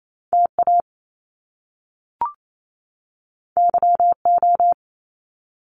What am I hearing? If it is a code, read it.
Morse code, heard as a keyed tone, about 28 wpm, 700 Hz. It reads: TA YO
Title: Top Two Letter Combinations in Words 28wpm